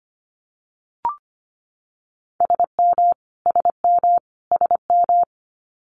Morse code, heard as a keyed tone, about 25 wpm, 700 Hz. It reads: SM SM SM
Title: Multi-letter Phonemes Repeated 3x 25wpm